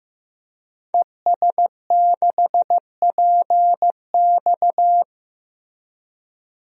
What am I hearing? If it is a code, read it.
Morse code, heard as a keyed tone, about 15 wpm, 700 Hz. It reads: ES6PX
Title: DXCC-100 Call Signs - 5 Characters - Repeated 3x - 15wpm